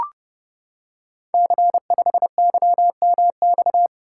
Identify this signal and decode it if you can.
Morse code, heard as a keyed tone, about 30 wpm, 700 Hz. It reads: C5YMX
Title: Code Groups - 5 Letter:Numbers - 30wpm